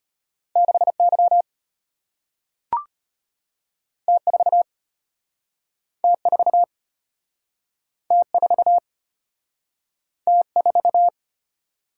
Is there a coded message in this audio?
Yes: Morse code from 0.6 to 11.1 s, heard as a keyed tone, about 34 wpm, 700 Hz. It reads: 6Y T4 T4 T4 T4